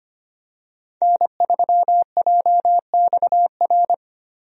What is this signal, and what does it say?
Morse code, heard as a keyed tone, about 25 wpm, 700 Hz. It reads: N3JXR
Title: USA Call Signs - 5 Characters - 25wpm